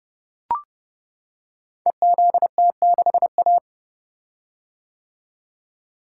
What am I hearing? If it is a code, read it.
Morse code, heard as a keyed tone, about 30 wpm, 700 Hz. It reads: EZT6A
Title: Code Groups - 5 Letter:Numbers - 30wpm